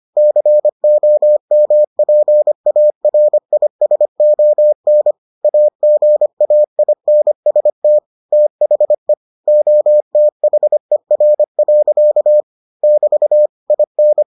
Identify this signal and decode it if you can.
Morse code, heard as a keyed tone, about 25 wpm, 600 Hz. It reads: COMPARISON AGAINST THE OTHER. <BT> IN